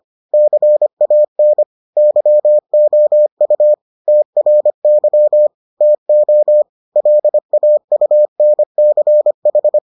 Morse code, heard as a keyed tone, about 25 wpm, 600 Hz. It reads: CAN YOU TRY TO LAUNCH